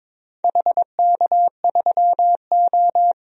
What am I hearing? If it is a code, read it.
Morse code, heard as a keyed tone, about 22 wpm, 700 Hz. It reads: HK3O
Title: DXCC-100 Call Signs - 4 Characters - Repeated 2x - 22wpm